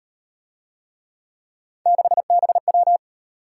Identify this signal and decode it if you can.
Morse code, heard as a keyed tone, about 38 wpm, 700 Hz. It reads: 6BW